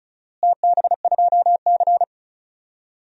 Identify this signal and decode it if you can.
Morse code, heard as a keyed tone, about 36 wpm, 700 Hz. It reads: TB2C
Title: DXCC-100 Call Signs - 4 Characters - Repeated 2x - 35wpm